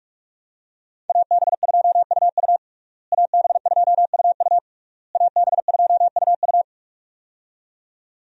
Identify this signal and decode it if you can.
Morse code, heard as a keyed tone, about 46 wpm, 700 Hz. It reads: AB2UU AB2UU AB2UU
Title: USA Call Signs - 5 Characters - Repeated 3x - 45wpm